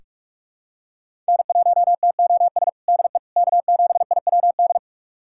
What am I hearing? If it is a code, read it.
Morse code, heard as a keyed tone, about 46 wpm, 700 Hz. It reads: N1TOS DE K7IWD
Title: USA Callsign DE Callsign - 5 Characters - 45wpm